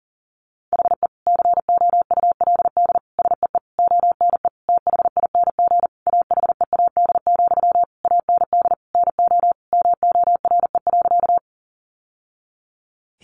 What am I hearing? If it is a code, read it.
Morse code, heard as a keyed tone, about 37 wpm, 700 Hz. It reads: HE COULD SEE ONE THING AHEAD, AND NO MORE.
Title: Sentences from Top 1100 Words 40wpm